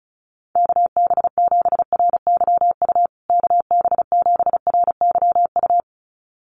Morse code, heard as a keyed tone, about 34 wpm, 700 Hz. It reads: KB7RYU KB7RYU
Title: USA Call Signs Repeated 2x 35wpm